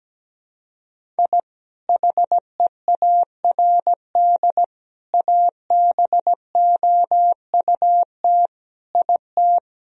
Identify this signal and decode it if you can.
Morse code, heard as a keyed tone, about 17 wpm, 700 Hz. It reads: I HEARD ABOUT IT
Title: Sentences from Top 1000 Words - Review Part 1 - 17wpm